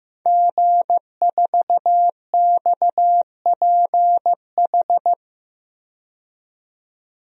Morse code, heard as a keyed tone, about 15 wpm, 700 Hz. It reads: G4XPH
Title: DXCC-100 Call Signs - 5 Characters - 15wpm